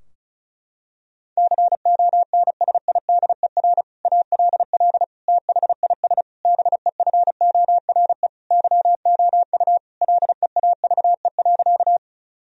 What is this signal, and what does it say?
Morse code, heard as a keyed tone, about 36 wpm, 700 Hz. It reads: CONSIDER ALL THIS BEFORE YOU LEAVE.
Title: Sentences from Top 1000 Words - Review Part 4 - 35wpm